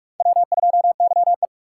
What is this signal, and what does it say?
Morse code, heard as a keyed tone, about 40 wpm, 700 Hz. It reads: W2YE